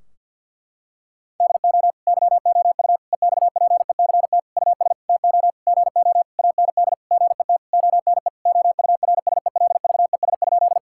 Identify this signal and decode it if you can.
Morse code, heard as a keyed tone, about 50 wpm, 700 Hz. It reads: DO YOU EXPECT US TO GO AND GET ONE OURSELVES?
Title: Sentences from Top 1600 Words 50wpm